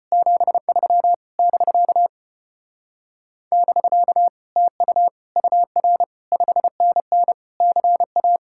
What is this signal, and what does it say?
Morse code, heard as a keyed tone, about 33 wpm, 700 Hz. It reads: 73 <BK> <BK> TU UR 5NN CA